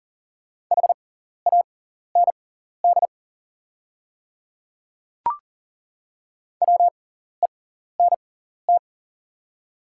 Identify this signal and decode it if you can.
Morse code, heard as a keyed tone, about 40 wpm, spaced as for 14 wpm, 700 Hz. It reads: HAND WENT
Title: 6x Character Spacing - Top 200 Words - 40wpm